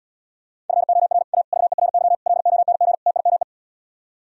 Morse code, heard as a keyed tone, about 77 wpm, 700 Hz. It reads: FOR A VERY LONG TIME